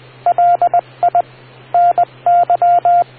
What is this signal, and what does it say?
Morse code, heard as a keyed tone, about 20 wpm, 700 Hz. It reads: LI NY